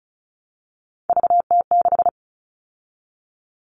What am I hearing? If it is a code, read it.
Morse code, heard as a keyed tone, about 34 wpm, 700 Hz. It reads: VT6
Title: All Cut Numbers - 3-Digit 35wpm